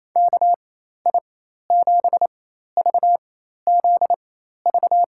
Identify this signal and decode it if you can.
Morse code, heard as a keyed tone, about 28 wpm, spaced as for 13 wpm, 700 Hz. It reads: KI7VZV
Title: USA Call Signs - 6 Characters - 4x Character Spacing - 28wpm